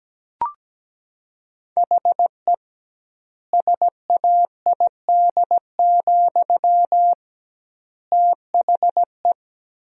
Morse code, heard as a keyed tone, about 17 wpm, 700 Hz. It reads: HE SAID, THE
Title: 2x Word-Spacing - Sentences from Top 400 Words - 17wpm